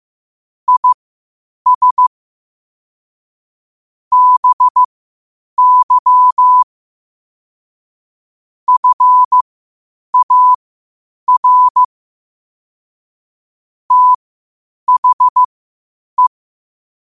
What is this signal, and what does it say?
Morse code, heard as a keyed tone, about 15 wpm, spaced as for 8 wpm, 1000 Hz. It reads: IS BY FAR THE
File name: qotdfiles / 8wpm